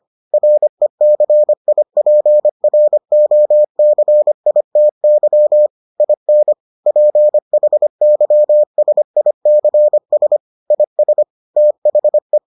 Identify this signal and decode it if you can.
Morse code, heard as a keyed tone, about 25 wpm, 600 Hz. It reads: RECIPROCITY IN PHYSICS IS THE